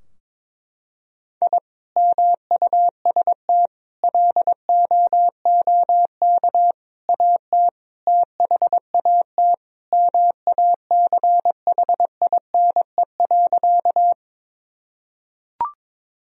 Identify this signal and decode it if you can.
Morse code, heard as a keyed tone, about 22 wpm, 700 Hz. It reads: I MUST LOOK AT THAT MACHINE.
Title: Sentences from Top 1000 Words - Review Part 2 - 22wpm